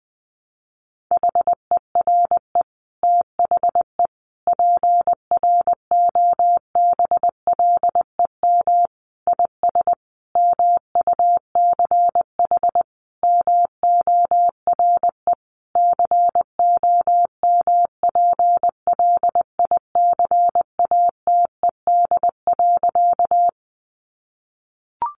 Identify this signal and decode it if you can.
Morse code, heard as a keyed tone, about 20 wpm, 700 Hz. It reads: HERE THE PROBLEM IS MUCH MORE COMPLICATED.
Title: Sentences from Top 1200 Words 20wpm